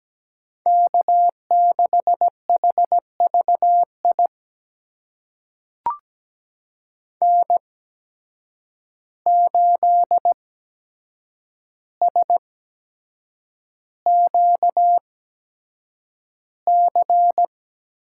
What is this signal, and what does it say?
Morse code, heard as a keyed tone, about 17 wpm, spaced as for 7 wpm, 700 Hz. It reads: K6HVI N8SQC
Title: USA Call Signs - 5 Characters - 8x Character Spacing - 17wpm